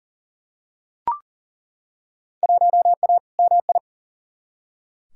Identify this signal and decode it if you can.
Morse code, heard as a keyed tone, about 40 wpm, 700 Hz. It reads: 1A MI
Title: Field Day Exchanges - 40wpm